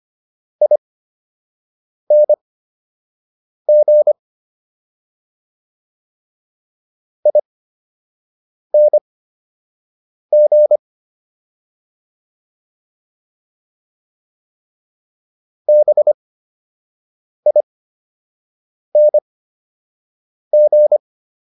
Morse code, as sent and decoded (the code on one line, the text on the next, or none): .. -. --. / .. -. --. / -... .. -. --.
ING ING BING